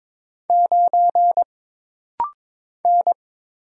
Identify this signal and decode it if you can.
Morse code, heard as a keyed tone, about 22 wpm, 700 Hz. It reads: 9 N
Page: This rapid-fire timing is geared to push your ICR (Instant Character Recognition) skills.